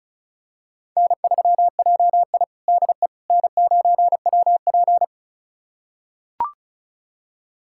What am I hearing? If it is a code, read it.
Morse code, heard as a keyed tone, about 36 wpm, 700 Hz. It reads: N3JI DE N9WP
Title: USA Callsign DE Callsign - 4 Characters - 35wpm